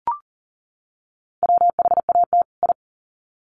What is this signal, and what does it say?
Morse code, heard as a keyed tone, about 40 wpm, 700 Hz. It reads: WHAT I
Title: Sets of 2 Words 40wpm